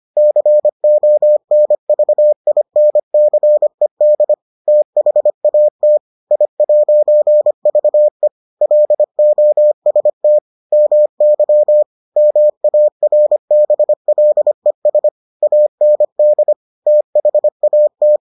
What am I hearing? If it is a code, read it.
Morse code, heard as a keyed tone, about 25 wpm, 600 Hz. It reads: CONVINCED THAT I'VE LOST MY MARBLES AND THAT